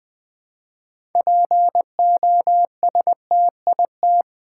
Morse code, heard as a keyed tone, about 20 wpm, 690 Hz. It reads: POSTIT
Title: Difficult Words with Hints - 20wpm